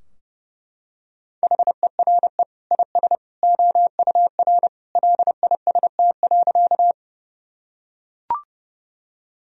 Morse code, heard as a keyed tone, about 30 wpm, 700 Hz. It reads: HERE IS OUR LIST.
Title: Sentences from Top 1000 Words - Review Part 2 - 30wpm